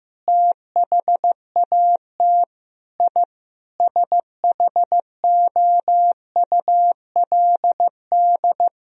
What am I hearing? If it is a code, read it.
Morse code, heard as a keyed tone, about 16 wpm, 700 Hz. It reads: THAT I SHOULD
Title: Sentences from Top 1000 Words - Review Part 2 - 15wpm